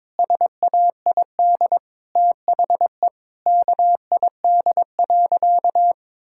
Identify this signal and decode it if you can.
Morse code, heard as a keyed tone, about 22 wpm, 700 Hz. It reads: SAID THE KID.
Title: Sentences from Top 1600 Words 22wpm